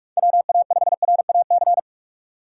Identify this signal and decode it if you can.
Morse code, heard as a keyed tone, about 46 wpm, 700 Hz. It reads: WA5RAC
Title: USA Call Signs - 2x Character Spacing - 45wpm